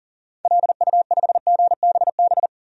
Morse code, heard as a keyed tone, about 40 wpm, 700 Hz. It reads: LU5GBB
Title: DXCC-100 Call Signs - 6 Characters - 40wpm